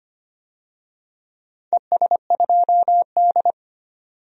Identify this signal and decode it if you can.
Morse code, heard as a keyed tone, about 25 wpm, 690 Hz. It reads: ES2D
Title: DXCC-100 Call Signs - 4 Characters - 25wpm